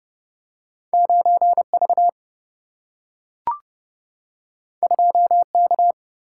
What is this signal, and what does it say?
Morse code, heard as a keyed tone, about 30 wpm, 700 Hz. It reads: 9V 2K
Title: Single-Letter Single-Number 30wpm